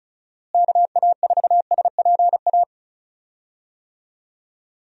Morse code, heard as a keyed tone, about 36 wpm, 700 Hz. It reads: KA4SPA
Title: USA Call Signs - 6 Characters - Repeated 2x - 35wpm